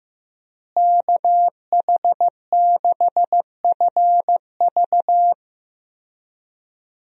Morse code, heard as a keyed tone, about 15 wpm, 700 Hz. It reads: KH6FV
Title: USA Call Signs - 5 Characters - Repeated 2x - 15wpm